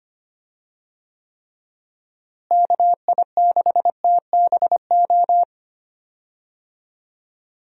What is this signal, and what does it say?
Morse code, heard as a keyed tone, about 25 wpm, 700 Hz. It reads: KI6TBO
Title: USA Call Signs - 6 Characters - Repeated 3x - 25wpm